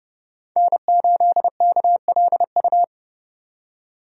Morse code, heard as a keyed tone, about 30 wpm, 700 Hz. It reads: N8KLU
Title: USA Call Signs - 5 Characters - 30wpm